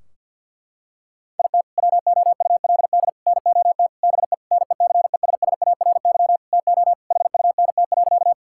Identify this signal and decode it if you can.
Morse code, heard as a keyed tone, about 50 wpm, 700 Hz. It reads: IT WOULD NOT BE NECESSARY TO HUNT.
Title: Sentences from Top 1600 Words 50wpm